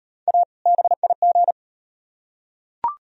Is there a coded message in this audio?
Yes: Morse code from 0.3 to 1.5 s, heard as a keyed tone, about 38 wpm, 700 Hz. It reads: A BIG